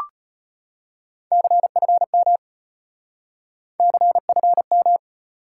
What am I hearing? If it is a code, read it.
Morse code, heard as a keyed tone, about 35 wpm, 700 Hz. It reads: CFM CFM